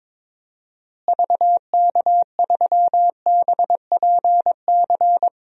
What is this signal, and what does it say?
Morse code, heard as a keyed tone, about 22 wpm, 700 Hz. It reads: VK3BPC
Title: DXCC-100 Call Signs - 6 Characters - Repeated 3x - 22wpm